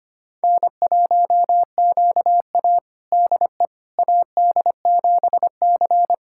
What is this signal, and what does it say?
Morse code, heard as a keyed tone, about 25 wpm, 700 Hz. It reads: N1QA DE AD7C
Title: USA Callsign DE Callsign - 4 Characters - 25wpm